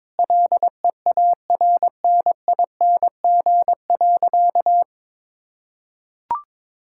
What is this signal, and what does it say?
Morse code, heard as a keyed tone, about 22 wpm, 700 Hz. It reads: LEARNING.
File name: Sentences with 7-Word Phrases - 22wpm